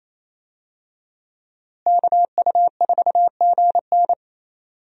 Morse code, heard as a keyed tone, about 28 wpm, 700 Hz. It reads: KU4GN
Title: USA Call Signs - 5 Characters - Repeated 3x - 28wpm